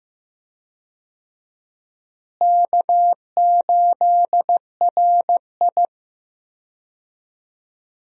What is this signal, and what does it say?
Morse code, heard as a keyed tone, about 15 wpm, 700 Hz. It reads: K8RI
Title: USA Call Signs - 4 Characters - Repeated 3x - 15wpm